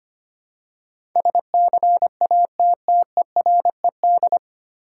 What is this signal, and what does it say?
Morse code, heard as a keyed tone, about 25 wpm, 700 Hz. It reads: SCATTERED
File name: practice / 25wpm / 9 Letter Words - First 3 Letters Given - 25wpm